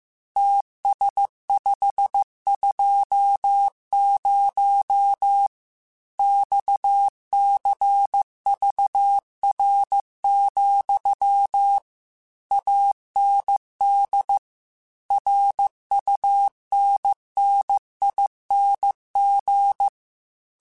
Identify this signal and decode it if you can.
Morse code, heard as a keyed tone, about 15 wpm, 800 Hz. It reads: TS520 XCVR, AND RUNNING
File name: QSOs / QSOs-mp3s / WD0FBT 1@13